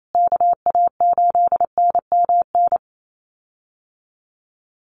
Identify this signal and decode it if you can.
Morse code, heard as a keyed tone, about 28 wpm, 690 Hz. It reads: KA8NMN
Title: USA Call Signs Repeated 3x 28wpm